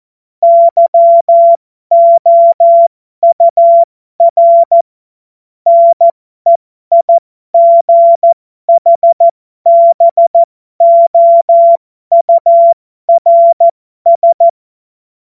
Morse code, heard as a keyed tone, about 14 wpm, spaced as for 12 wpm, 680 Hz. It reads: YOUR NEIGHBOURS
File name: Morsekurs / Witze